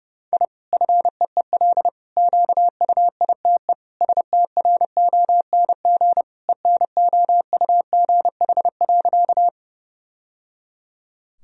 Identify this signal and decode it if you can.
Morse code, heard as a keyed tone, about 30 wpm, 700 Hz. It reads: I FEEL QUITE STRONG ENOUGH.
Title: Sentences from Top 1000 Words - Review Part 3 - 30wpm